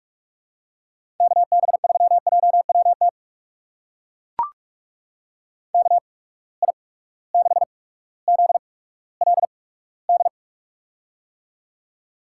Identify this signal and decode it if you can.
Morse code, heard as a keyed tone, about 43 wpm, spaced as for 19 wpm, 700 Hz. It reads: KB3JWT KI6ZLD
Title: USA Call Signs - 6 Characters - 8x Character Spacing - 45wpm